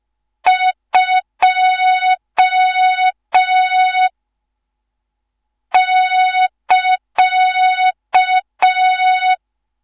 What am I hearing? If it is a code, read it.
Morse code, heard as a keyed tone, about 5 wpm, 3700 Hz, with noise. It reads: 2 <KA>
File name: Morze-HA2MN / hang / 14 280 HA2MN